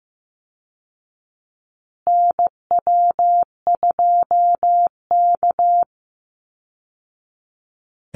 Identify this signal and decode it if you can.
Morse code, heard as a keyed tone, about 15 wpm, 700 Hz. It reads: NW2K
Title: Call Signs Repeated 2x - 15wpm